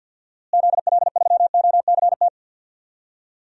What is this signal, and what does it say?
Morse code, heard as a keyed tone, about 52 wpm, 700 Hz. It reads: ZL3OCT
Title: DXCC-100 Call Signs - 6 Characters - Repeated 3x - 50wpm